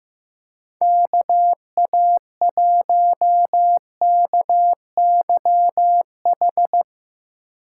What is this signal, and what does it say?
Morse code, heard as a keyed tone, about 15 wpm, 700 Hz. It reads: K A 1 K Y H
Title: USA Call Signs - 8x Character Spacing - 15wpm